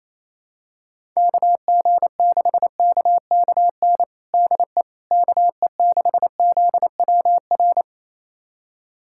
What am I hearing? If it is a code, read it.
Morse code, heard as a keyed tone, about 27 wpm, 700 Hz. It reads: KG6KKN DE KE6ZWR
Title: USA Callsign DE Callsign - 6 Characters - 28wpm